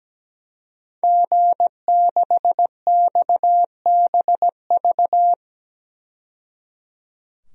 Morse code, heard as a keyed tone, about 17 wpm, 700 Hz. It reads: G6XBV
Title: DXCC-100 Call Signs - 5 Characters - 17wpm